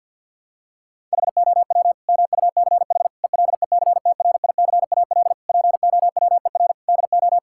Morse code, heard as a keyed tone, about 50 wpm, 700 Hz. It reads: HOW MUCH ELECTRICAL POWER DO
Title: Sentences from Top 1000 Words - Review Part 6 - 50wpm